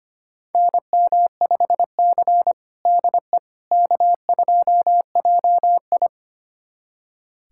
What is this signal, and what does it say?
Morse code, heard as a keyed tone, about 25 wpm, 700 Hz. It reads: NM5C DE K2JI
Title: USA Callsign DE Callsign - 4 Characters - 25wpm